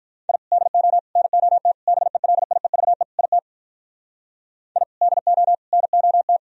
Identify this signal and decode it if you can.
Morse code, heard as a keyed tone, about 45 wpm, 700 Hz. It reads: I DO NOT BELIEVE IT I DO NOT